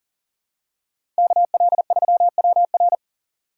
Morse code, heard as a keyed tone, about 40 wpm, 700 Hz. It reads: KL3WR
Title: USA Call Signs - 6x Character Spacing - 40wpm